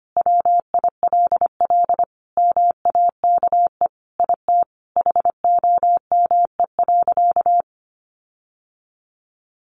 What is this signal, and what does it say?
Morse code, heard as a keyed tone, about 25 wpm, 700 Hz. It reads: WILL MAKE IT HOME.
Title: Sentences from Top 1500 Words 25wpm